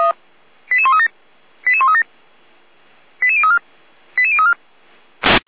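Five-tone selective call (ZVEI), heard as quick runs of five tones, tones of about 70 ms. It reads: 88137 88137 88034 88034
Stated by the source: ZVEI five-tone selective call received by SDR.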